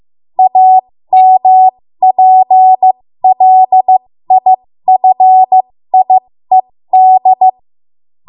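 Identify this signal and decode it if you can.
Morse code, heard as a keyed tone, about 15 wpm, spaced as for 13 wpm, 750 Hz. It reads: AMPLIFIED
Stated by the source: 13 wpm Morse code practice archive transmission.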